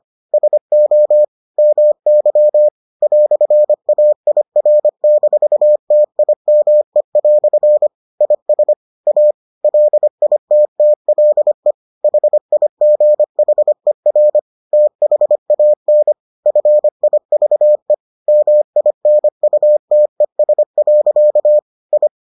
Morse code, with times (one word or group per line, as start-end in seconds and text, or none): 0.3-1.2 SO
1.6-2.7 MY
3.0-7.9 "AIR-TIME"
8.2-8.7 IS
9.1-9.3 A
9.6-11.7 LITTLE
12.0-14.4 HIGHER
14.7-16.1 THAN
16.5-17.9 FIVE
18.3-21.6 MINUTES.
21.9-22.1 I